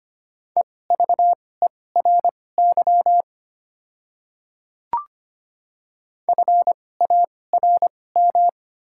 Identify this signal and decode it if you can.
Morse code, heard as a keyed tone, about 25 wpm, spaced as for 18 wpm, 700 Hz. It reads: EVERY FARM
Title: 2x Character Spacing - Top 500 Words - 25wpm